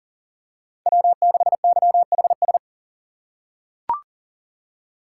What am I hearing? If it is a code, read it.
Morse code, heard as a keyed tone, about 40 wpm, 700 Hz. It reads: W6YHS
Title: USA Call Signs - 6x Character Spacing - 40wpm